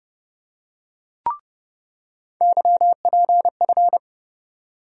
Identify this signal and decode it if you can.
Morse code, heard as a keyed tone, about 30 wpm, 700 Hz. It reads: YPF
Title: USA Call Sign Postfixes - 3 Letters 30wpm